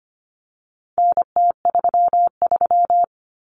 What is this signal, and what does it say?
Morse code, heard as a keyed tone, about 25 wpm, 700 Hz. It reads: NT33
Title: Common Cut Numbers - 4-Digit 25wpm